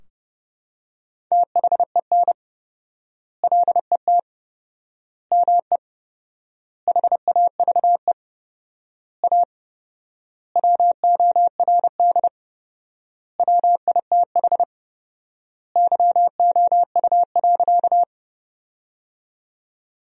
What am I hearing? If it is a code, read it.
Morse code, heard as a keyed tone, about 30 wpm, 700 Hz. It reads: THEN LET ME HAVE A WORD WITH YOU.
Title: 4x Word-Spacing - Sentences from Top 300 Words - 30wpm